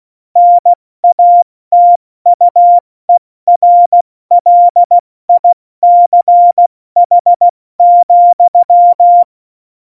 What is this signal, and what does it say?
Morse code, heard as a keyed tone, about 16 wpm, 700 Hz. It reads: NATUERLICH,
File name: Morsekurs / Witze / Witz 24. November 2020, 70 F80